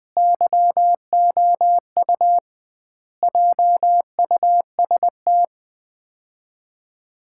Morse code, heard as a keyed tone, about 20 wpm, 700 Hz. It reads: YOU JUST
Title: 2x Word-Spacing - Sets of 5 Words - 20wpm